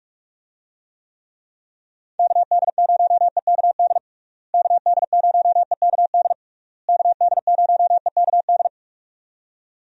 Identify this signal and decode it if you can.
Morse code, heard as a keyed tone, about 46 wpm, 700 Hz. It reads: KD0EKD KD0EKD KD0EKD
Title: USA Call Signs - 6 Characters - Repeated 3x - 45wpm